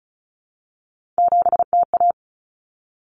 Morse code, heard as a keyed tone, about 34 wpm, 700 Hz. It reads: 7TA
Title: Common Cut Numbers - 3-Digit 35wpm